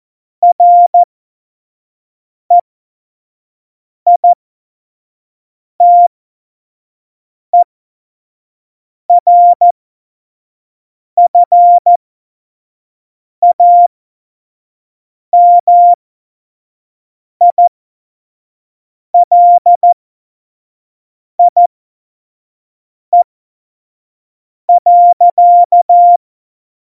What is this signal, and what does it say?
Morse code, heard as a keyed tone, about 14 wpm, spaced as for 5 wpm, 700 Hz. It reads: REITERFAMILIE.
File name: Morsekurs / Witze / Witz 04 August 2020, 25 F70